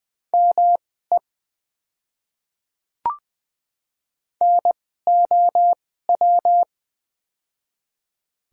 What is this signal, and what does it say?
Morse code, heard as a keyed tone, about 20 wpm, spaced as for 14 wpm, 700 Hz. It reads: ME NOW
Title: 2x Character Spacing - Top 200 Words - 20wpm